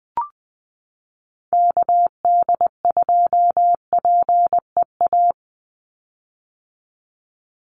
Morse code, heard as a keyed tone, about 20 wpm, 700 Hz. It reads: KD2PEA
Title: USA Call Signs 20wpm